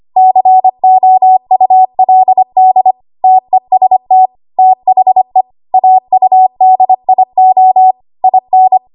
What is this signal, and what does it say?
Morse code, heard as a keyed tone, about 25 wpm, 750 Hz. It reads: COULD TEST THE AUDIO IN
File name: Archive / 25 WPM / 25WPM